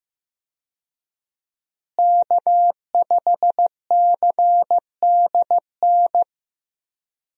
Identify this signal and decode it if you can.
Morse code, heard as a keyed tone, about 15 wpm, 700 Hz. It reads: K5CDN
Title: USA Call Signs - 5 Characters - Repeated 3x - 15wpm